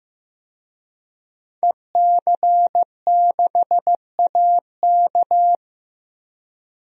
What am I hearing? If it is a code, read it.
Morse code, heard as a keyed tone, about 15 wpm, 700 Hz. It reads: EC6AK
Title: DXCC-100 Call Signs - 5 Characters - Repeated 2x - 15wpm